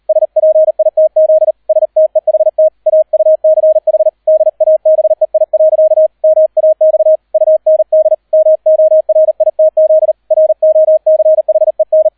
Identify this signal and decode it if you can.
Morse code, heard as a keyed tone, about 38 wpm, 630 Hz, with radio noise. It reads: SPITZ STEHT AUCH DABEI. MAX UND MORITZ ROCHEN